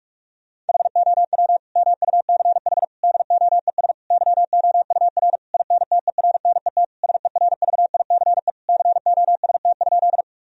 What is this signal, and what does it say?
Morse code, heard as a keyed tone, about 46 wpm, 700 Hz. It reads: HOW MUCH DOES YOUR INTERNET SERVICE COST?
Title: Sentences from Top 1000 Words - Review Part 4 - 45wpm